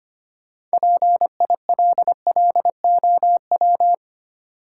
Morse code, heard as a keyed tone, about 25 wpm, 700 Hz. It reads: PILLOW
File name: practice / 25wpm / Easy Words with Hints - 25wpm